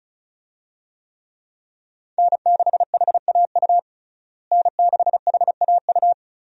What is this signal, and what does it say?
Morse code, heard as a keyed tone, about 34 wpm, 700 Hz. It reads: N6HAU N6HAU
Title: USA Call Signs - 5 Characters - Repeated 2x - 35wpm